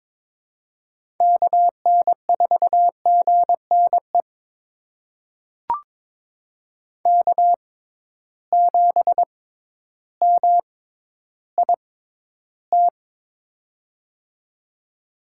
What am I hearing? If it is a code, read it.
Morse code, heard as a keyed tone, about 22 wpm, spaced as for 12 wpm, 700 Hz. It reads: KN4GNE K7MIT
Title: USA Call Signs - 6x Character Spacing - 22wpm